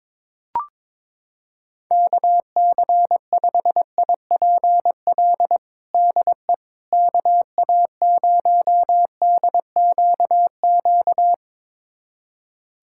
Morse code, heard as a keyed tone, about 22 wpm, 700 Hz. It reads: KC5IPL DE KA0DQQ
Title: USA Callsign DE Callsign - 6 Characters - 22wpm